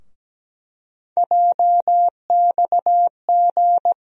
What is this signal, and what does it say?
Morse code, heard as a keyed tone, about 17 wpm, 700 Hz. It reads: JXG
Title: USA Call Sign Postfixes - 3 Letters 17wpm